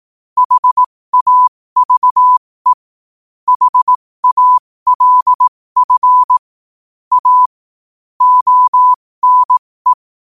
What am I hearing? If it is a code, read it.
Morse code, heard as a keyed tone, about 18 wpm, spaced as for 15 wpm, 1000 Hz. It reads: HAVE HALF A ONE